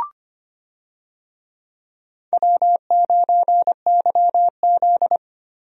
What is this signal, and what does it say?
Morse code, heard as a keyed tone, about 25 wpm, 700 Hz. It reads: W9YZ
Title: USA Call Signs - 4 Characters - Repeated 3x - 25wpm